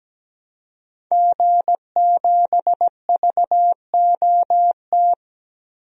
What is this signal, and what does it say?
Morse code, heard as a keyed tone, about 17 wpm, 700 Hz. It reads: G7VOT
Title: DXCC-100 Call Signs - 5 Characters - 17wpm